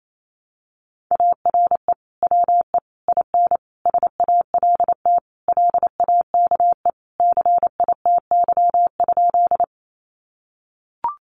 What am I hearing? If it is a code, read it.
Morse code, heard as a keyed tone, about 28 wpm, 700 Hz. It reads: ARE WE IN SALT LAKE CITY?
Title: Sentences from Top 1100 Words 28wpm